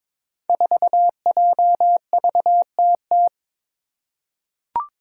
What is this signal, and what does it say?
Morse code, heard as a keyed tone, about 23 wpm, 700 Hz. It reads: 4JVTT
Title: Five Challenging Letter-Number Combinations 22wpm